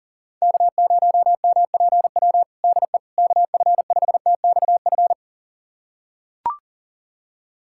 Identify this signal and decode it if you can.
Morse code, heard as a keyed tone, about 37 wpm, 700 Hz. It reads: K0MPW DE KF5TXF
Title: USA Callsign DE Callsign 40wpm